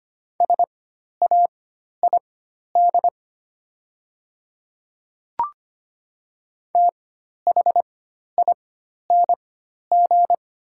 Morse code, heard as a keyed tone, about 25 wpm, spaced as for 12 wpm, 700 Hz. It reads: SAID THING
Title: 4x Character Spacing - Top 100 Words - 25wpm